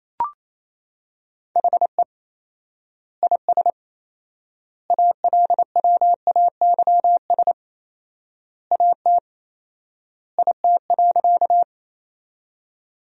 Morse code, heard as a keyed tone, about 28 wpm, 700 Hz. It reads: HE IS ALWAYS AT IT.
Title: 4x Word-Spacing - Sentences from Top 300 Words - 28wpm